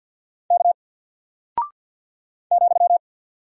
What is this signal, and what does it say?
Morse code, heard as a keyed tone, about 48 wpm, 700 Hz. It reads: K ,
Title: Single Character (Letter:Number:Punctuation:Prosign) Alt 50wpm